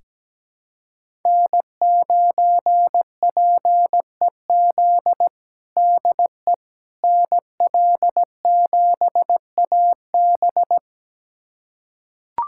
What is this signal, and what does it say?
Morse code, heard as a keyed tone, about 17 wpm, 700 Hz. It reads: N9PEZ DE NL7AB
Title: USA Callsign DE Callsign - 5 Characters - 17wpm